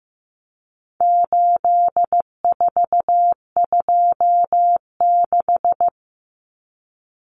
Morse code, heard as a keyed tone, about 15 wpm, 700 Hz. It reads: 8426
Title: Numbers - 4-Digit 15wpm